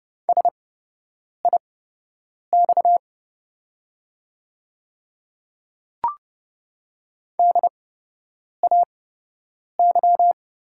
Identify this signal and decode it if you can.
Morse code, heard as a keyed tone, about 30 wpm, spaced as for 8 wpm, 690 Hz. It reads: SIX DAY
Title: 8x Character Spacing - Top 500 Words - 30wpm